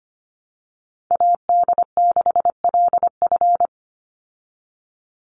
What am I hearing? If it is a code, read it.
Morse code, heard as a keyed tone, about 25 wpm, 700 Hz. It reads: AD6LF
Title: USA Call Signs 25wpm